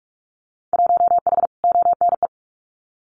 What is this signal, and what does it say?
Morse code, heard as a keyed tone, about 46 wpm, 700 Hz. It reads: 1H ONE
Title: Winter Field Day Exchanges Encore 45wpm